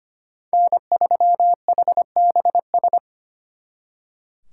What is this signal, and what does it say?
Morse code, heard as a keyed tone, about 33 wpm, 700 Hz. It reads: N3HBS